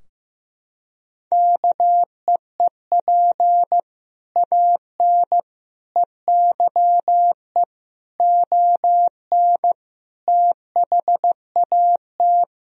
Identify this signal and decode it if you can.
Morse code, heard as a keyed tone, about 15 wpm, 700 Hz. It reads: KEEP AN EYE ON THAT
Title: Sentences from Top 1000 Words - Review Part 2 - 15wpm